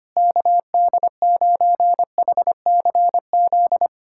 Morse code, heard as a keyed tone, about 25 wpm, 700 Hz. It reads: KD9HCZ
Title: USA Callsign DE Callsign - 6 Characters - 25wpm